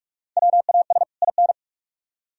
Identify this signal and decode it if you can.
Morse code, heard as a keyed tone, about 43 wpm, 700 Hz. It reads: WAS IN